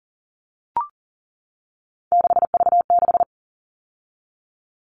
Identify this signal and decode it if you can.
Morse code, heard as a keyed tone, about 37 wpm, 700 Hz. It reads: 6V6
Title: All Cut Numbers - 3-Digit 40wpm